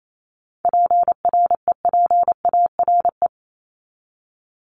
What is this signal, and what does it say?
Morse code, heard as a keyed tone, about 28 wpm, 700 Hz. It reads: PREPARE
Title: ED Suffix - Top 100 Words with Base Words 28wpm